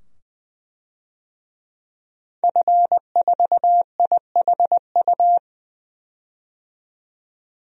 Morse code, heard as a keyed tone, about 20 wpm, 700 Hz. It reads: F4IHU
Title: DXCC-100 Call Signs - 5 Characters - 20wpm